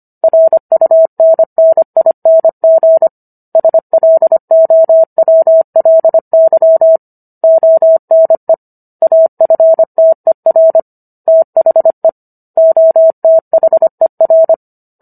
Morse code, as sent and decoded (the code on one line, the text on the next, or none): .-. ..- -. -. .. -. --. / ... .-.. --- .-- .-.. -.-- / --- -. / .- ..-. - . .-. / - .... / --- - .... .-.
RUNNING SLOWLY ONE AFTER THE OTHER